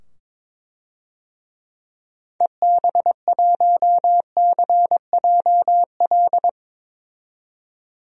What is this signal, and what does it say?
Morse code, heard as a keyed tone, about 22 wpm, 700 Hz. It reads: EB1CJL
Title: DXCC-100 Call Signs - 6 Characters - 22wpm